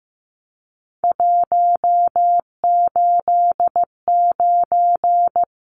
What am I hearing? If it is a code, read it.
Morse code, heard as a keyed tone, about 15 wpm, 700 Hz. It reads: 189
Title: Numbers - 3-Digit 15wpm